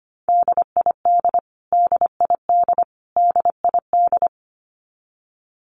Morse code, heard as a keyed tone, about 25 wpm, 700 Hz. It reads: DID DID DID
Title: Top 300 Words Repeated 25wpm